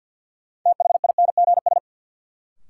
Morse code, heard as a keyed tone, about 53 wpm, 700 Hz. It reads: THINGS
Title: Sets of 3 Words - New Word Drill - 50wpm